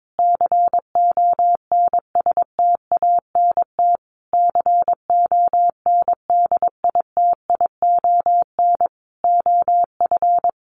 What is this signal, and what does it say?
Morse code, heard as a keyed tone, about 22 wpm, 700 Hz. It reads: CONSTANT CONDITION OF
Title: Sentences from Top 1500 Words 22wpm